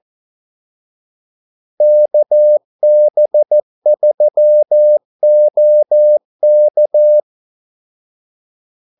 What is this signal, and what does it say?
Morse code, heard as a keyed tone, about 14 wpm, 600 Hz. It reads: KB3OK
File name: Callsigns-14wpm